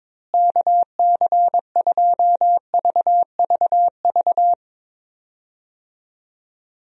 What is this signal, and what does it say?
Morse code, heard as a keyed tone, about 22 wpm, 700 Hz. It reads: KC2VVV
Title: USA Call Signs - 6 Characters - Repeated 2x - 22wpm